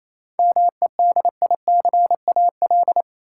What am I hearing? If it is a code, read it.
Morse code, heard as a keyed tone, about 28 wpm, 700 Hz. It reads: MEDICAL
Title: Sentences from Top 800 Words - New Word Drill - 28wpm